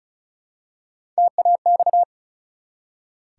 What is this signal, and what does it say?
Morse code, heard as a keyed tone, about 36 wpm, 700 Hz. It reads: TAX
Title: Three Letter Words - 35wpm